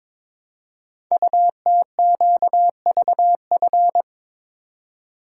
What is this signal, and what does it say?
Morse code, heard as a keyed tone, about 22 wpm, 700 Hz. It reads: UTQVF
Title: Code Groups - 5 Letters - 22wpm